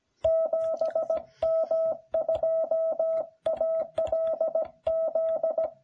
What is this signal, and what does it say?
Morse code, heard as a keyed tone, about 16 wpm, 660 Hz. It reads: 7M2ALZ